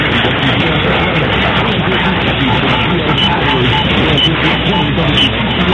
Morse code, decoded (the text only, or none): none